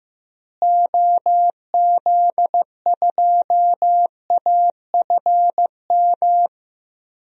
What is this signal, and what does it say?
Morse code, heard as a keyed tone, about 15 wpm, 700 Hz. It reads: OZ2AFM
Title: DXCC-100 Call Signs - 6 Characters - Repeated 2x - 15wpm